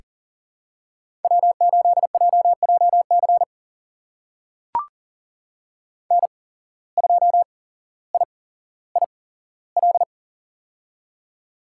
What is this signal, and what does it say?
Morse code, heard as a keyed tone, about 39 wpm, spaced as for 17 wpm, 700 Hz. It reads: W8JJC N2IIL
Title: USA Call Signs - 5 Characters - 8x Character Spacing - 40wpm